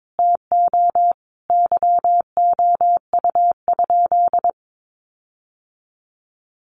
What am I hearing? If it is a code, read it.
Morse code, heard as a keyed tone, about 22 wpm, 700 Hz. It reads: TO YOU?
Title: Sentences from Top 1300 Words 22wpm